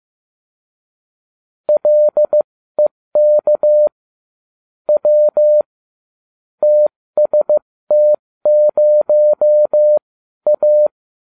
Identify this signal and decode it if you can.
Morse code, heard as a keyed tone, about 15 wpm, spaced as for 13 wpm, 600 Hz. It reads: LEK W TST0A